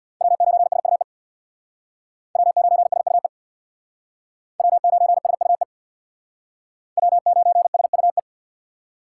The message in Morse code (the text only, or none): W9SUE W9SUE W9SUE W9SUE